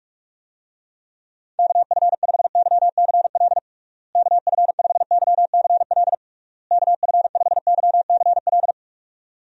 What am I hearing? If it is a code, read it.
Morse code, heard as a keyed tone, about 46 wpm, 700 Hz. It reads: KF5YCL KF5YCL KF5YCL
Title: USA Call Signs - 6 Characters - Repeated 3x - 45wpm